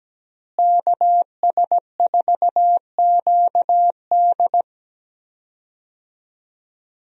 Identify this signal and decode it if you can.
Morse code, heard as a keyed tone, about 17 wpm, 700 Hz. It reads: KS4QD
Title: USA Call Signs - 5 Characters - Repeated 2x - 17wpm